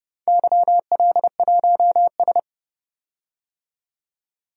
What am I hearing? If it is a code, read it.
Morse code, heard as a keyed tone, about 31 wpm, 700 Hz. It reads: YL1S
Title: DXCC-100 Call Signs - 4 Characters - 30wpm